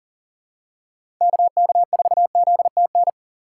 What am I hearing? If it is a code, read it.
Morse code, heard as a keyed tone, about 39 wpm, 700 Hz. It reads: KK4ZTN
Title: USA Call Signs - 6 Characters - 4x Character Spacing - 40wpm